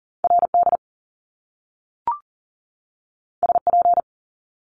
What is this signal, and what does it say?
Morse code, heard as a keyed tone, about 40 wpm, 700 Hz. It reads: RD SP
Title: Top Two Letter Combinations in Words 40wpm